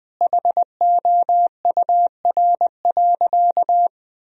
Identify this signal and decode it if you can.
Morse code, heard as a keyed tone, about 21 wpm, 700 Hz. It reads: HOUR.
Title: Sentences from Top 1000 Words - Review Part 2 - 20wpm